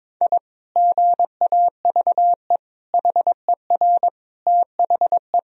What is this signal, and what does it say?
Morse code, heard as a keyed tone, about 22 wpm, 700 Hz. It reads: I GAVE HER THE
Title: Sentences from Top 1000 Words - Review Part 3 - 22wpm